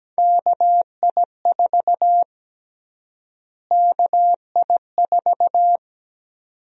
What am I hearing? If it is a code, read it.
Morse code, heard as a keyed tone, about 17 wpm, 700 Hz. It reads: KI4 KI4
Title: USA Call Sign Prefixes - Repeated 2x - 17wpm